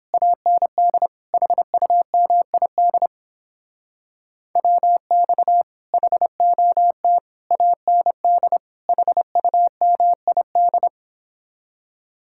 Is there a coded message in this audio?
Yes: Morse code from 0.1 to 10.9 s, heard as a keyed tone, about 29 wpm, 700 Hz. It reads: AND HUMID WX HOT AND HUMID